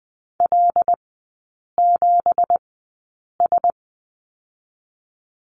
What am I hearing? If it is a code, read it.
Morse code, heard as a keyed tone, about 20 wpm, spaced as for 8 wpm, 690 Hz. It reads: L7S
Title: 198 Lesson 32e - Letter k - Three Characters 20wpm